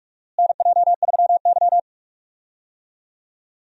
Morse code, heard as a keyed tone, about 46 wpm, 700 Hz. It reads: NJ3Y
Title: USA Call Signs - 4 Characters - Repeated 3x - 45wpm